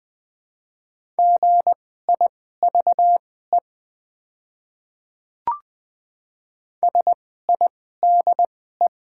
Morse code, heard as a keyed tone, about 20 wpm, spaced as for 14 wpm, 700 Hz. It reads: GIVE SIDE
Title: 2x Character Spacing - Top 200 Words - 20wpm